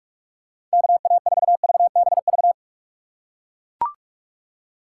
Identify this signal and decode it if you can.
Morse code, heard as a keyed tone, about 41 wpm, 700 Hz. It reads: KA4VBV
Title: USA Call Signs - 6 Characters - 6x Character Spacing - 45wpm